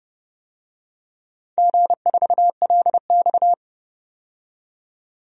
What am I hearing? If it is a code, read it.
Morse code, heard as a keyed tone, about 30 wpm, 700 Hz. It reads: G4LX
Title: DXCC-100 Call Signs - 4 Characters - 30wpm